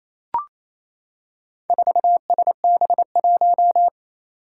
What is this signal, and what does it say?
Morse code, heard as a keyed tone, about 29 wpm, 700 Hz. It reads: 4SB1
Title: Four Challenging Letter-Number Combinations 28wpm